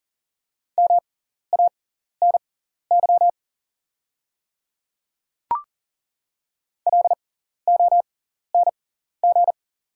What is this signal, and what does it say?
Morse code, heard as a keyed tone, about 40 wpm, spaced as for 14 wpm, 690 Hz. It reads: MANY LONG
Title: 6x Character Spacing - Top 200 Words - 40wpm